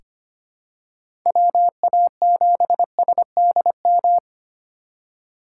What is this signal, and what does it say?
Morse code, heard as a keyed tone, about 25 wpm, 690 Hz. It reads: WA7SDM
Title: USA Call Signs - 6 Characters - 3x Character Spacing - 25wpm